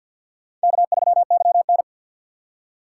Morse code, heard as a keyed tone, about 52 wpm, 700 Hz. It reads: K3YN
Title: USA Call Signs - 4 Characters - 6x Character Spacing - 50wpm